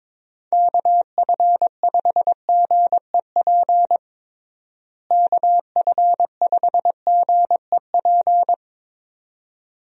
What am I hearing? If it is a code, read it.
Morse code, heard as a keyed tone, about 22 wpm, 700 Hz. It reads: KF5GEP KF5GEP
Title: USA Call Signs - 6 Characters - Repeated 2x - 22wpm